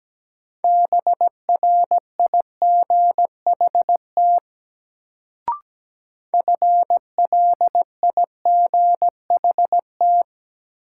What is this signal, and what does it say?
Morse code, heard as a keyed tone, about 17 wpm, 700 Hz. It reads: BRIGHT FLIGHT
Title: Sentences from Top 1000 Words - New Word Drill - 17wpm